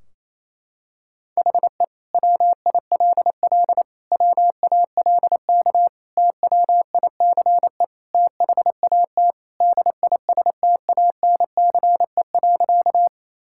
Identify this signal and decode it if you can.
Morse code, heard as a keyed tone, about 28 wpm, 700 Hz. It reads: HE WILL WALK TWICE THAT DISTANCE.
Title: Sentences from Top 1600 Words 28wpm